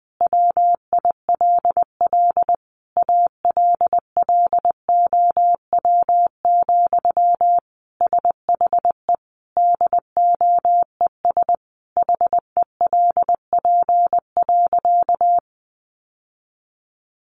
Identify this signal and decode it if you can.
Morse code, heard as a keyed tone, about 20 wpm, 700 Hz. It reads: WILL ALLOW, SHE DOES HELP.
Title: Sentences from Top 700 Words 20wpm